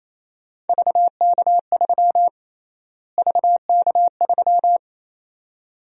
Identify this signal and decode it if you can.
Morse code, heard as a keyed tone, about 28 wpm, 700 Hz. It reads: VK3 VK3
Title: USA Call Sign Prefixes - Repeated 2x - 28wpm